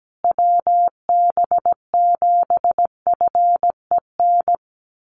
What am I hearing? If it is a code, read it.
Morse code, heard as a keyed tone, about 17 wpm, 700 Hz. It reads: WB7FEN
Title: USA Call Signs Repeated 3x 17wpm